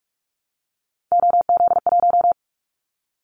Morse code, heard as a keyed tone, about 44 wpm, 700 Hz. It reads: OZ1
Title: Callsign Prefixes for Top 100 DXCC - 45wpm